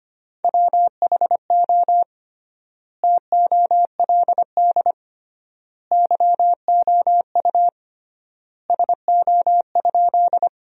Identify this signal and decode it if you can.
Morse code, heard as a keyed tone, about 25 wpm, 700 Hz. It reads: WHO TOLD YOU SO?
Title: 3x Word-Spacing - Sentences from Top 400 Words - 25wpm